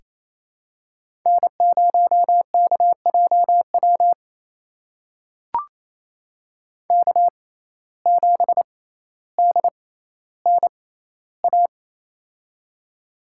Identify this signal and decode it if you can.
Morse code, heard as a keyed tone, about 28 wpm, spaced as for 14 wpm, 700 Hz. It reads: N0KJW K7DNA
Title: USA Call Signs - 5 Characters - 6x Character Spacing - 28wpm